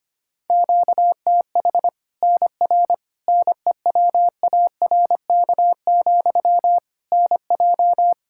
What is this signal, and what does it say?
Morse code, heard as a keyed tone, about 25 wpm, 700 Hz. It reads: QTH NR NEWARK, NJ
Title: Practice QSO - Exchange 1 - Beginning - 25wpm